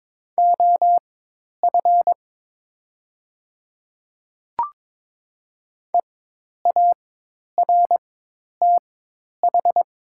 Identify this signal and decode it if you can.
Morse code, heard as a keyed tone, about 22 wpm, spaced as for 10 wpm, 700 Hz. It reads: OF EARTH
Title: 4x Character Spacing - Top 200 Words - 22wpm